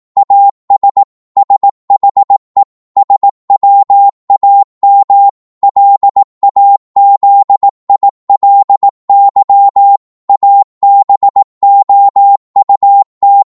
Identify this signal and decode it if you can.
Morse code, heard as a keyed tone, about 18 wpm, 800 Hz. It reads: AS SHE SWAM LAZILY ABOUT
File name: Morse / psg-18wpm